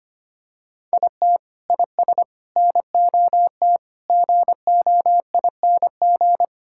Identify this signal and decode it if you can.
Morse code, heard as a keyed tone, about 25 wpm, 700 Hz. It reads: IT IS NOT GOING
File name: practice / 25wpm / Sentences with 4-Word Phrases - 25wpm